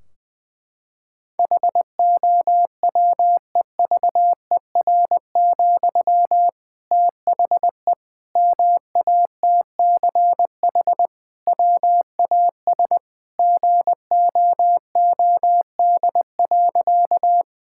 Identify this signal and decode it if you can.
Morse code, heard as a keyed tone, about 21 wpm, 700 Hz. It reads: HOWEVER, THE MATCH WAS GOOD.
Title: Sentences from Top 1600 Words 20wpm